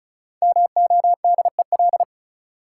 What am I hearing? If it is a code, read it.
Morse code, heard as a keyed tone, about 36 wpm, 700 Hz. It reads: MODEL
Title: Sentences from Top 1000 Words - New Word Drill - 35wpm